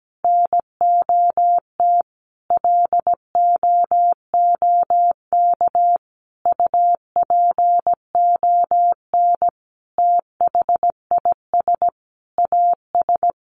Morse code, heard as a keyed tone, about 17 wpm, 700 Hz. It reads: NOT LOOK UPON THIS AS
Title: Sentences from Top 1100 Words 17wpm